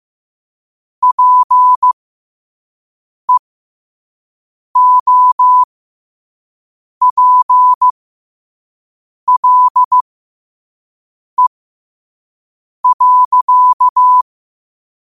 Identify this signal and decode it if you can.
Morse code, heard as a keyed tone, about 15 wpm, spaced as for 5 wpm, 1000 Hz. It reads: PEOPLE.
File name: qotdfiles / 5wpm